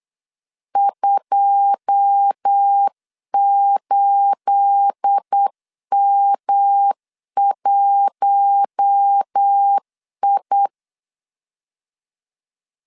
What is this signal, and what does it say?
Morse code, heard as a keyed tone, about 8 wpm, 800 Hz. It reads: 28M1I
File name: LEZIONI / 9 28 8wpm 3512